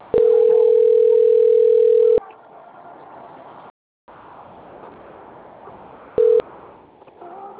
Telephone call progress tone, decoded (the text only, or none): ringback tone